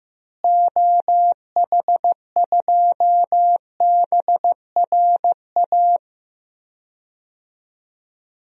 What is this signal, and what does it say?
Morse code, heard as a keyed tone, about 15 wpm, 700 Hz. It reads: OH2BRA
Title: DXCC-100 Call Signs - 6 Characters - Repeated 2x - 15wpm